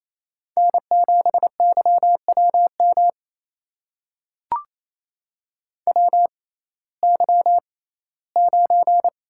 Morse code, heard as a keyed tone, about 28 wpm, spaced as for 17 wpm, 700 Hz. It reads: N7YWM WY9
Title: USA Call Signs - 6x Character Spacing - 28wpm